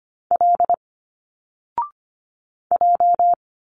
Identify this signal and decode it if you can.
Morse code, heard as a keyed tone, about 25 wpm, 700 Hz. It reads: L J
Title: Single Letters 25wpm